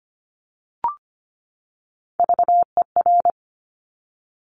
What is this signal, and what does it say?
Morse code, heard as a keyed tone, about 25 wpm, 700 Hz. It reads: VER
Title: Top Three Letter Combinations in Words 25wpm